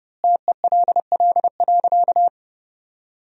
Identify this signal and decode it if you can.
Morse code, heard as a keyed tone, about 30 wpm, 700 Hz. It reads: TELL.
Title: 4x Word-Spacing - Sentences from Top 300 Words - 30wpm